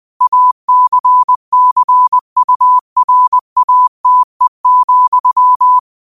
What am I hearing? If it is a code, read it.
Morse code, heard as a keyed tone, about 20 wpm, 1000 Hz. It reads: ACCURATE,
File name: qotdfiles / 20wpm